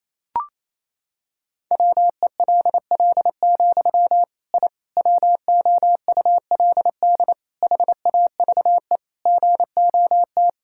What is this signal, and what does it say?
Morse code, heard as a keyed tone, about 28 wpm, 700 Hz. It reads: WELL, I WOULD HAVE GOT
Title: Sentences from Top 1000 Words - Review Part 5 - 28wpm